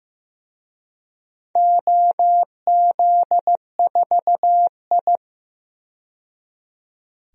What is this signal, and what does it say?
Morse code, heard as a keyed tone, about 15 wpm, 700 Hz. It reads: OZ4I
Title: DXCC-100 Call Signs - 4 Characters - 15wpm